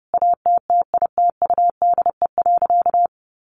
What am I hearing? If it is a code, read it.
Morse code, heard as a keyed tone, about 30 wpm, 700 Hz. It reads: ATTITUDE.
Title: Sentences from Top 1300 Words 30wpm